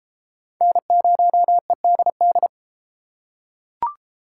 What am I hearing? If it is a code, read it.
Morse code, heard as a keyed tone, about 30 wpm, 700 Hz. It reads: N0EDD